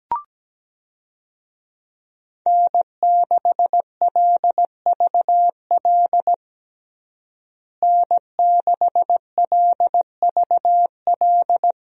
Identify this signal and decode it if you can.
Morse code, heard as a keyed tone, about 17 wpm, 700 Hz. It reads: N6LVL N6LVL
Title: USA Call Signs - 5 Characters - Repeated 2x - 17wpm